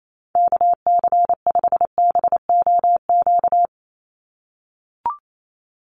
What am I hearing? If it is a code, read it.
Morse code, heard as a keyed tone, about 28 wpm, 700 Hz. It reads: KC5BOQ
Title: USA Call Signs 28wpm